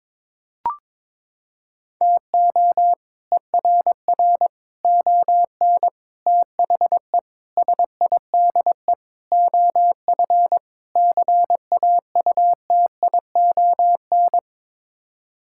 Morse code, heard as a keyed tone, about 22 wpm, 700 Hz. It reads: TO ERR ON THE SIDE OF CAUTION
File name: practice / 22wpm / Sets of 7 Words - 22wpm